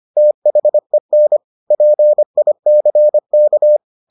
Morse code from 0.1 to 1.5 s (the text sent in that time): THEN